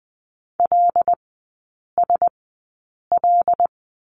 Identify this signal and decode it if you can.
Morse code, heard as a keyed tone, about 20 wpm, spaced as for 8 wpm, 690 Hz. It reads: LSL
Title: 087 Lesson 15e - Number 5 - Three Characters 20wpm